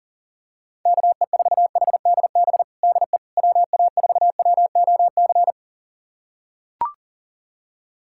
Morse code, heard as a keyed tone, about 40 wpm, 700 Hz. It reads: KE4HDB DE WA4WOC
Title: USA Callsign DE Callsign - 6 Characters - 40wpm